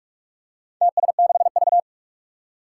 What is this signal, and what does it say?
Morse code, heard as a keyed tone, about 44 wpm, 700 Hz. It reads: TS6V
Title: Four Challenging Letter-Number Combinations 45wpm